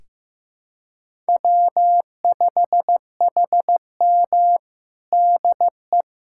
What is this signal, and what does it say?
Morse code, heard as a keyed tone, about 15 wpm, 700 Hz. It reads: W5HM DE
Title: USA Callsign DE Callsign - 4 Characters - 15wpm